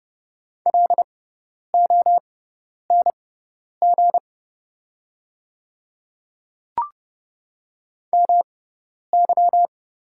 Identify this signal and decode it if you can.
Morse code, heard as a keyed tone, about 31 wpm, spaced as for 10 wpm, 700 Hz. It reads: LONG MY
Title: 6x Character Spacing - Top 100 Words - 30wpm